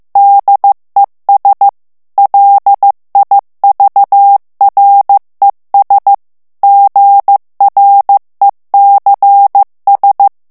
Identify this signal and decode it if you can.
Morse code, heard as a keyed tone, about 15 wpm, 800 Hz. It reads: DES LIVRES GRECS
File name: morse mp3 / FRANCAIS TEXTE CLAIR 2 VITESSE 14 MOTS MINUTES